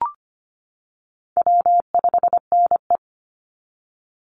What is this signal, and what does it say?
Morse code, heard as a keyed tone, about 25 wpm, 700 Hz. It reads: W5NE
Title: Call Signs - 25wpm